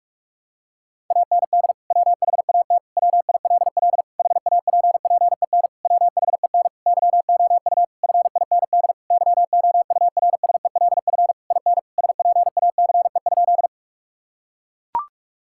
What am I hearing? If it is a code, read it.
Morse code, heard as a keyed tone, about 41 wpm, 700 Hz. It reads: AND WHAT WILL HAPPEN WHEN YOU FIND YOURSELF IN SPACE?
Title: Sentences from Top 1000 Words - Review Part 5 - 45wpm